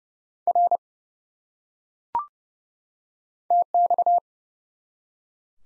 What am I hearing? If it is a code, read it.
Morse code, heard as a keyed tone, about 30 wpm, 700 Hz. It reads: R TX
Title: Top 90 Abbr 30WPM